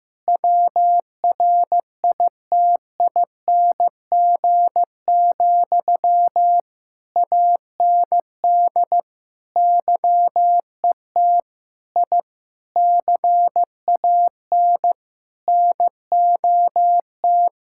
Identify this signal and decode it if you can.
Morse code, heard as a keyed tone, about 15 wpm, 700 Hz. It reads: WRITING, AND YET I CAN NOT
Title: Sentences from Top 1000 Words - Review Part 6 - 15wpm